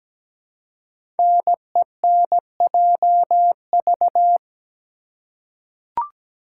Morse code, heard as a keyed tone, about 17 wpm, 700 Hz. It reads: NENJV
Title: Code Groups - 5 Letters - 17wpm